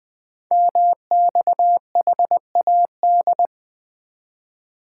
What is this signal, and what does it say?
Morse code, heard as a keyed tone, about 20 wpm, 700 Hz. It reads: MXHAD
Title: Code Groups - 5 Letters - 20wpm